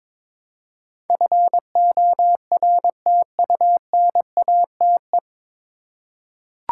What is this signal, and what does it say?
Morse code, heard as a keyed tone, about 22 wpm, 700 Hz. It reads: FORTUNATE
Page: Morse Code Ninja practice 22wpm